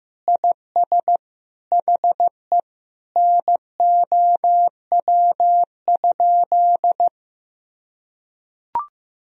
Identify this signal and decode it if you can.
Morse code, heard as a keyed tone, about 16 wpm, 700 Hz. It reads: IS HE NOW?
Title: Sentences from Top 200 Words and Abbreviations 15wpm